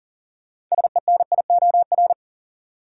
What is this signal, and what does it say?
Morse code, heard as a keyed tone, about 40 wpm, 700 Hz. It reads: SENIOR
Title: Sets of 5 Words Encore - New Word Drill - 40wpm